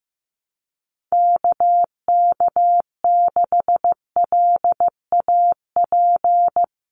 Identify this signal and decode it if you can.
Morse code, heard as a keyed tone, about 15 wpm, 700 Hz. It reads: KK6LAP
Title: USA Call Signs Repeated 2x 15wpm